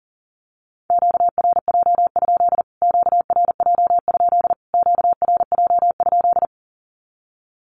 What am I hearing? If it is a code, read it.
Morse code, heard as a keyed tone, about 40 wpm, 700 Hz. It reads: QRJ? QRJ? QRJ?
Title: Q-Codes Repeated 3x 40wpm